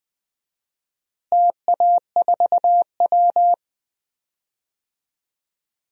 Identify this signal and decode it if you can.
Morse code, heard as a keyed tone, about 20 wpm, 700 Hz. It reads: TA4W
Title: DXCC-100 Call Signs - 4 Characters - Repeated 3x - 20wpm